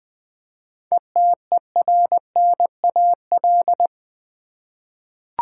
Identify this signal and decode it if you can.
Morse code, heard as a keyed tone, about 20 wpm, 700 Hz. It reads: ETERNAL
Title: 7 Letter Words - First 3 Letters Given - 20wpm